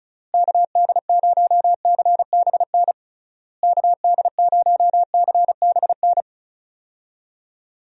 Morse code, heard as a keyed tone, about 36 wpm, 700 Hz. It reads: KD0CBN KD0CBN
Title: USA Call Signs - 6 Characters - Repeated 3x - 35wpm